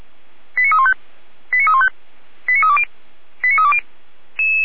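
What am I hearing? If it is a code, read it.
Five-tone selective call (ZVEI), heard as quick runs of five tones, tones of about 70 ms. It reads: 89315 89315 89320 89320
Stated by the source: ZVEI five-tone selective call received by SDR, address 89313.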